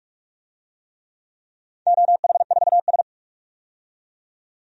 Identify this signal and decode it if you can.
Morse code, heard as a keyed tone, about 47 wpm, 700 Hz. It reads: OH4S
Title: DXCC-100 Call Signs - 4 Characters - 45wpm